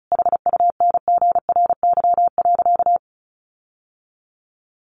Morse code, heard as a keyed tone, about 34 wpm, 700 Hz. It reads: HUNGRY.
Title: Sentences from Top 1400 Words 35wpm